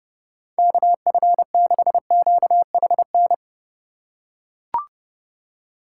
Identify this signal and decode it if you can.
Morse code, heard as a keyed tone, about 30 wpm, 700 Hz. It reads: KF6QHN
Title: USA Call Signs - 2x Character Spacing - 30wpm